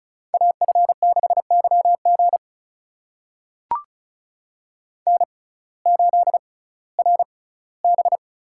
Morse code, heard as a keyed tone, about 36 wpm, spaced as for 19 wpm, 700 Hz. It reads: AF6YG N8RB
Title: USA Call Signs - 5 Characters - 6x Character Spacing - 35wpm